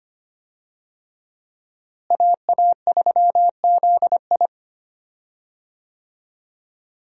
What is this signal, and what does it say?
Morse code, heard as a keyed tone, about 25 wpm, 700 Hz. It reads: AA3ZI
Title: USA Call Signs - 5 Characters - Repeated 3x - 25wpm